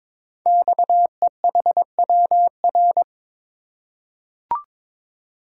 Morse code, heard as a keyed tone, about 22 wpm, 700 Hz. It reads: XEHWR
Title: Code Groups - 5 Letters - 22wpm